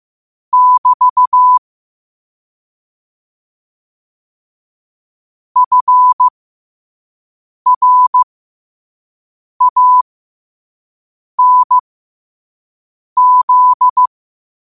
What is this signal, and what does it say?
Morse code, heard as a keyed tone, about 15 wpm, spaced as for 5 wpm, 1000 Hz. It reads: <BT> FRANZ
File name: 5wpm